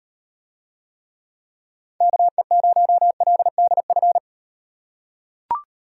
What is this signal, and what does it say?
Morse code, heard as a keyed tone, about 38 wpm, 700 Hz. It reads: KE0LDF